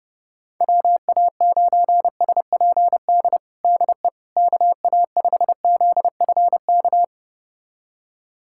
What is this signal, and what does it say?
Morse code, heard as a keyed tone, about 32 wpm, 700 Hz. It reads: WA9SPD DE KA5ZFK
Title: USA Callsign DE Callsign - 6 Characters - 30wpm